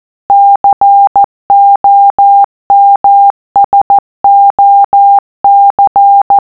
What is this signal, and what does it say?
Morse code, heard as a keyed tone, about 14 wpm, 800 Hz. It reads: COMSOC